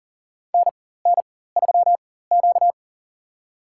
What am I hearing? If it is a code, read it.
Morse code, heard as a keyed tone, about 37 wpm, spaced as for 18 wpm, 700 Hz. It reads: NN3Q
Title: USA Call Signs - 4 Characters - 4x Character Spacing - 40wpm